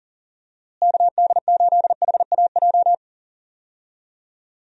Morse code, heard as a keyed tone, about 40 wpm, 700 Hz. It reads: KD8HAJ
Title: USA Call Signs - 6 Characters - Repeated 3x - 40wpm